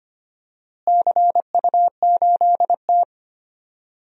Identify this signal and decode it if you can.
Morse code, heard as a keyed tone, about 25 wpm, 700 Hz. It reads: CU8T
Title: DXCC-100 Call Signs - 4 Characters - Repeated 3x - 25wpm